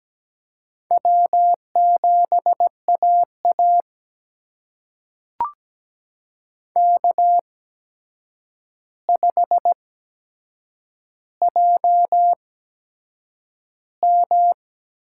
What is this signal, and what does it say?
Morse code, heard as a keyed tone, about 17 wpm, spaced as for 7 wpm, 700 Hz. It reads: W7AA K5JM
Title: USA Call Signs - 8x Character Spacing - 17wpm